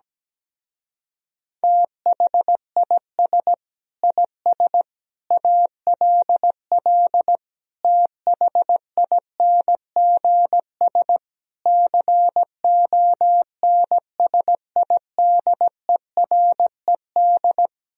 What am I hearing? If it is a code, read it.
Morse code, heard as a keyed tone, about 17 wpm, 700 Hz. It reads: THIS IS ALL THINGS CONSIDERED
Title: Speed-Racing - Sets of 5 Words 17wpm